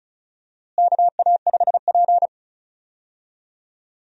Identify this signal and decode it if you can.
Morse code, heard as a keyed tone, about 36 wpm, 700 Hz. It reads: KA5P
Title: USA Call Signs - 4 Characters - Repeated 2x - 35wpm